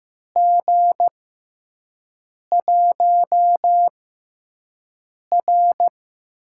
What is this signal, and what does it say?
Morse code, heard as a keyed tone, about 15 wpm, spaced as for 5 wpm, 700 Hz. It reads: G1R
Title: USA Call Signs - 4 Characters - 6x Character Spacing - 15wpm